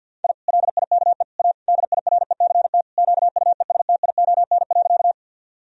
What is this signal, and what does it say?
Morse code, heard as a keyed tone, about 50 wpm, 700 Hz. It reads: I LIKE A DIRECT QUESTION.
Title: Sentences from Top 1000 Words - Review Part 2 - 50wpm